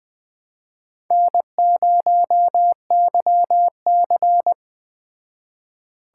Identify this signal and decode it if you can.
Morse code, heard as a keyed tone, about 20 wpm, 700 Hz. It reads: N0YC